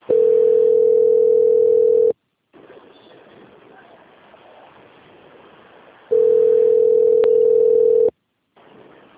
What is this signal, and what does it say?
Ringback tone (North America), 440 + 480 Hz.